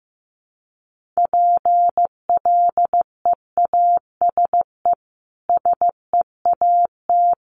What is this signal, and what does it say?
Morse code, heard as a keyed tone, about 15 wpm, 700 Hz. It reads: PLEASE SEAT
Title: Sentences from Top 1500 Words 15wpm